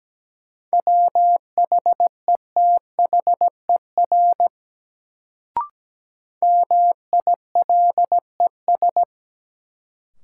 Morse code, heard as a keyed tone, about 17 wpm, 700 Hz. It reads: WHETHER MILES
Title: Sentences from Top 1000 Words - Review Part 3 - New Word Drill - 17wpm